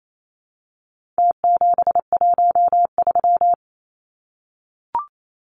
Morse code, heard as a keyed tone, about 28 wpm, 700 Hz. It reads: T713
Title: Numbers - Contest Style - 4-Digit 28wpm